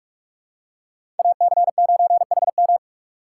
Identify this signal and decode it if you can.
Morse code, heard as a keyed tone, about 45 wpm, 700 Hz. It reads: AC9HM
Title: USA Call Signs - 2x Character Spacing - 45wpm